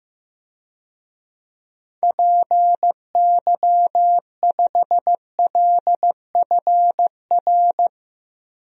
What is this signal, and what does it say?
Morse code, heard as a keyed tone, about 16 wpm, 700 Hz. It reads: PY5LFR
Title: DXCC-100 Call Signs - 6 Characters - Repeated 2x - 15wpm